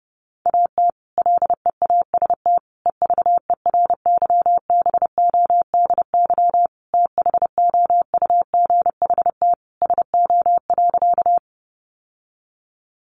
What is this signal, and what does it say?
Morse code, heard as a keyed tone, about 30 wpm, 700 Hz. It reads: AT LEAST EVERYBODY THOUGHT SO.
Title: Sentences from Top 1400 Words 30wpm